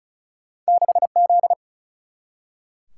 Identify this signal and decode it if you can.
Morse code, heard as a keyed tone, about 36 wpm, 700 Hz. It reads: 6Z
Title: Single-Letter Single-Number 35wpm